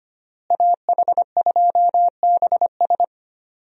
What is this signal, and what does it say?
Morse code, heard as a keyed tone, about 25 wpm, 700 Hz. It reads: AH2BS
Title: USA Call Signs - 5 Characters - Repeated 3x - 25wpm